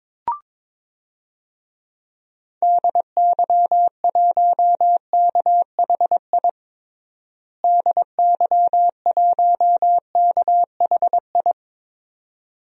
Morse code, heard as a keyed tone, about 22 wpm, 700 Hz. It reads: DY1KHI DY1KHI
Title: DXCC-100 Call Signs - 6 Characters - Repeated 2x - 22wpm